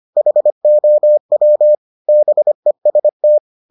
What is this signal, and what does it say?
Morse code, heard as a keyed tone, about 25 wpm, 600 Hz. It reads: HOW BEST